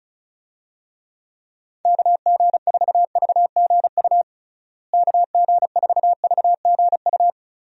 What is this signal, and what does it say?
Morse code, heard as a keyed tone, about 34 wpm, 700 Hz. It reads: KG4VGU KG4VGU
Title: USA Call Signs - 6 Characters - Repeated 2x - 35wpm